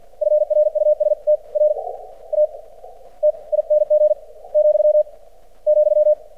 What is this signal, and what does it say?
Morse code, heard as a keyed tone, about 48 wpm, spaced as for 26 wpm, 600 Hz. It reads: FRUSTRATING , ,
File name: archived / CRSnet